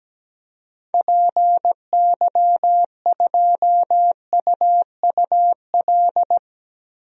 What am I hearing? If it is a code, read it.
Morse code, heard as a keyed tone, about 17 wpm, 700 Hz. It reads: PY2UUL
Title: DXCC-100 Call Signs - 6 Characters - 17wpm